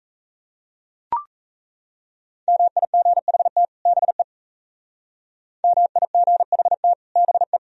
Morse code, heard as a keyed tone, about 36 wpm, 700 Hz. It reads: MIGHT BE MIGHT BE